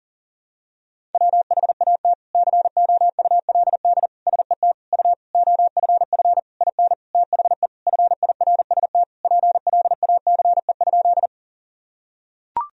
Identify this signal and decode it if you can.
Morse code, heard as a keyed tone, about 40 wpm, 700 Hz. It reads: WHAT COULD SET U OFF IN THE FIRST PLACE?
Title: Sentences from Top 200 Words and Abbreviations 40wpm